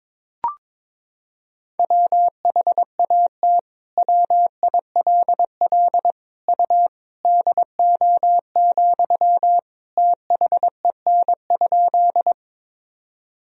Morse code, heard as a keyed tone, about 22 wpm, 700 Hz. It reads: WHAT WILL U DO, THEN?
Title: Sentences from Top 100 Words and Abbreviations 22wpm